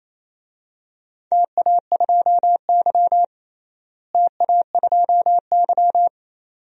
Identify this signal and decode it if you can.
Morse code, heard as a keyed tone, about 28 wpm, 700 Hz. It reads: TA2Y TA2Y
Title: DXCC-100 Call Signs - 4 Characters - Repeated 2x - 28wpm